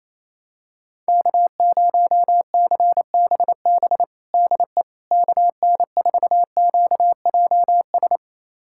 Morse code, heard as a keyed tone, about 28 wpm, 700 Hz. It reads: K0CBB DE KN4QJS
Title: USA Callsign DE Callsign 28wpm